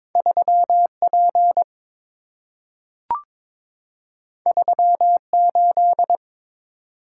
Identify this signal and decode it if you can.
Morse code, heard as a keyed tone, about 22 wpm, 700 Hz. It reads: 3P 38
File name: practice / 22wpm / Two Letter-Number Combinations - 22wpm